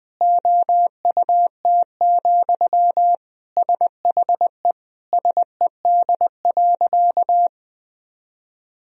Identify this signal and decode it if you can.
Morse code, heard as a keyed tone, about 20 wpm, 700 Hz. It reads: OUT, SHE SED.
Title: Sentences from Top 500 Words and Abbreviations 20wpm